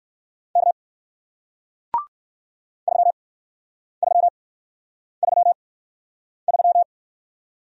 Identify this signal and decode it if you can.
Morse code, heard as a keyed tone, about 57 wpm, spaced as for 9 wpm, 700 Hz. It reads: K 3333